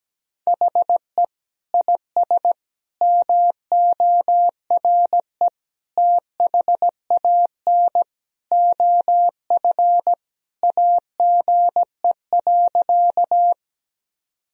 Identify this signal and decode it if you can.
Morse code, heard as a keyed tone, about 17 wpm, 700 Hz. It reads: HE IS MORE THAN OF AGE.
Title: Sentences from Top 1000 Words - Review Part 3 - 17wpm